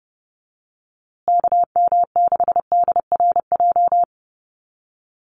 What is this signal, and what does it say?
Morse code, heard as a keyed tone, about 30 wpm, 700 Hz. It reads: KM6DRJ
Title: USA Call Signs Repeated 3x 30wpm